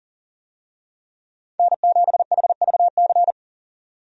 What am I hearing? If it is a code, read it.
Morse code, heard as a keyed tone, about 40 wpm, 700 Hz. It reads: N7HVC
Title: USA Call Signs - 5 Characters - 40wpm